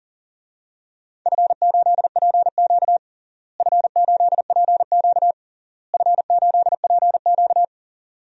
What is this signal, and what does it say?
Morse code, heard as a keyed tone, about 40 wpm, 700 Hz. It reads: F8PQ F8PQ F8PQ
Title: DXCC-100 Call Signs - 4 Characters - Repeated 3x - 40wpm